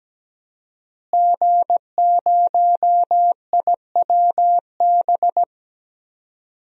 Morse code, heard as a keyed tone, about 17 wpm, 700 Hz. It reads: G0IWB
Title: DXCC-100 Call Signs - 5 Characters - Repeated 2x - 17wpm